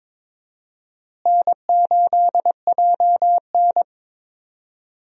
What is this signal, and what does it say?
Morse code, heard as a keyed tone, about 22 wpm, 700 Hz. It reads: N8JN
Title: USA Call Signs - 4 Characters - Repeated 2x - 22wpm